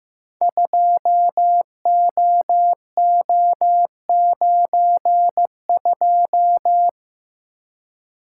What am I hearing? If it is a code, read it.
Morse code, heard as a keyed tone, about 15 wpm, 700 Hz. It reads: 2OO92
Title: Code Groups - 5 Letter:Numbers - 15wpm